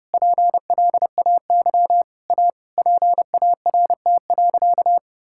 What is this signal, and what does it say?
Morse code, heard as a keyed tone, about 32 wpm, 700 Hz. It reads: PLAY A PART.
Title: Sentences from Top 1000 Words - Review Part 5 - 30wpm